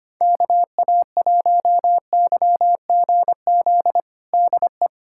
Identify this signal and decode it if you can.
Morse code, heard as a keyed tone, about 25 wpm, 700 Hz. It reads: KA1YGZ DE
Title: USA Callsign DE Callsign - 6 Characters - 25wpm